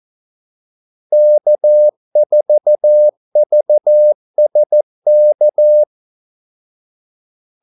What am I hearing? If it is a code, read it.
Morse code, heard as a keyed tone, about 14 wpm, 600 Hz. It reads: K4VSK
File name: Callsigns-14wpm